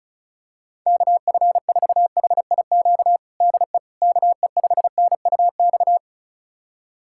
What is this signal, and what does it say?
Morse code, heard as a keyed tone, about 36 wpm, 700 Hz. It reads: KF4HIQ DE KE5NUX
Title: USA Callsign DE Callsign 35wpm